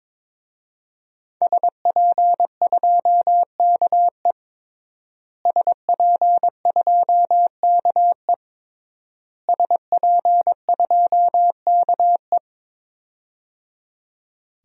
Morse code, heard as a keyed tone, about 22 wpm, 700 Hz. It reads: SP2KE SP2KE SP2KE
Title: DXCC-100 Call Signs - 5 Characters - Repeated 3x - 22wpm